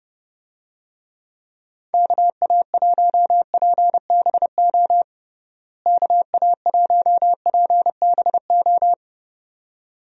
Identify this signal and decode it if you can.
Morse code, heard as a keyed tone, about 30 wpm, 700 Hz. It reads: KA1PBO KA1PBO
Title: USA Call Signs - 6 Characters - Repeated 2x - 30wpm